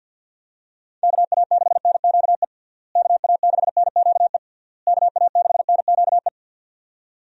Morse code, heard as a keyed tone, about 51 wpm, 700 Hz. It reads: KA6NQE KA6NQE KA6NQE
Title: USA Call Signs - 6 Characters - Repeated 3x - 50wpm